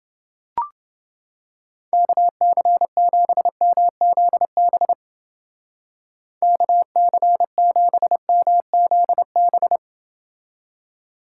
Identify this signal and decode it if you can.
Morse code, heard as a keyed tone, about 30 wpm, 700 Hz. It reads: KC7MZB KC7MZB